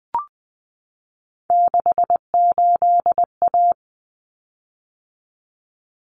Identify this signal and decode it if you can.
Morse code, heard as a keyed tone, about 20 wpm, 700 Hz. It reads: 68A
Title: Common Cut Numbers - 3-Digit 20wpm